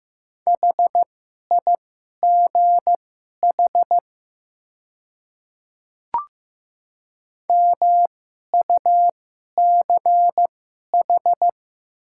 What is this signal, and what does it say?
Morse code, heard as a keyed tone, about 15 wpm, spaced as for 11 wpm, 700 Hz. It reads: HIGH MUCH
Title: 2x Character Spacing - Top 200 Words - 15wpm